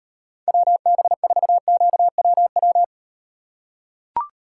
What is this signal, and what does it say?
Morse code, heard as a keyed tone, about 38 wpm, 700 Hz. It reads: WB4QWW